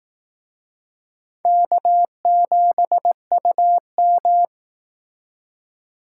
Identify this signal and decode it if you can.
Morse code, heard as a keyed tone, about 18 wpm, 700 Hz. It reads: K7UM
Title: Speed Racing - USA 1x2 Call Signs 15wpm